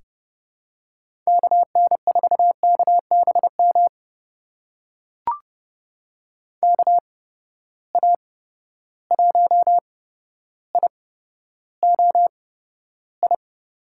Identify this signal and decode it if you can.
Morse code, heard as a keyed tone, about 30 wpm, spaced as for 13 wpm, 700 Hz. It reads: KN4KBM KA1IOI
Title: USA Call Signs - 6 Characters - 8x Character Spacing - 30wpm